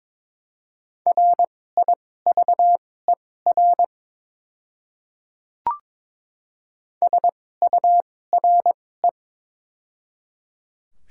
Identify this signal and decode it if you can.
Morse code, heard as a keyed tone, about 22 wpm, spaced as for 16 wpm, 700 Hz. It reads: RIVER SURE
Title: 2x Character Spacing - Top 300 Words - 22wpm